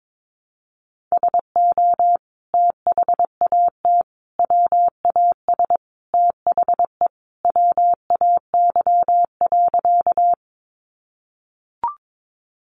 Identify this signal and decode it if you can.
Morse code, heard as a keyed tone, about 22 wpm, 700 Hz. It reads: SO THAT WAS THE WAY.
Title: Sentences from Top 100 Words 22wpm